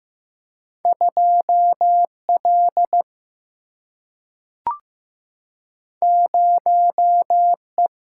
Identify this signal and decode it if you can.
Morse code, heard as a keyed tone, about 15 wpm, 700 Hz. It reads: 2 L 0 E
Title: Single-Letter Single-Number 15wpm